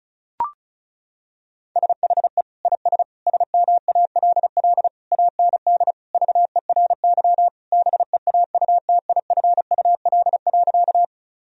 Morse code, heard as a keyed tone, about 36 wpm, 700 Hz. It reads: SHE IS SMALL AND VERY BEAUTIFUL.
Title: Sentences from Top 1000 Words - Review Part 1 - 35wpm